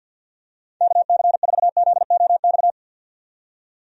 Morse code, heard as a keyed tone, about 52 wpm, 700 Hz. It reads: KC4ZOX
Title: USA Call Signs - 6 Characters - 50wpm